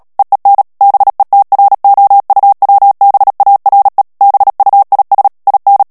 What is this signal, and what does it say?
Morse code, heard as a keyed tone, about 37 wpm, 800 Hz. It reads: EEN BETROUWBARE BUIS IN
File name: morse mp3 / HOLLANDAIS texte clair 2 vitesse 35 mots minute